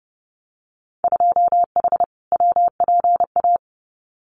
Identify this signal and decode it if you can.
Morse code, heard as a keyed tone, about 30 wpm, 700 Hz. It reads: 2H WPA
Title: Winter Field Day Exchanges 30wpm